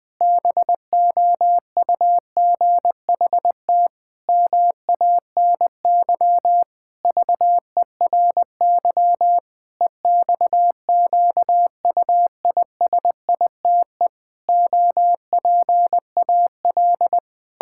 Morse code, heard as a keyed tone, about 21 wpm, 700 Hz. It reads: BOUGHT MANY VERY EXQUISITE OPAL
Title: Pangram Challenge v2 - 20wpm